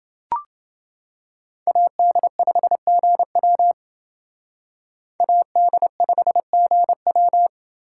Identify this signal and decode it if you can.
Morse code, heard as a keyed tone, about 28 wpm, 700 Hz. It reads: AD5GW AD5GW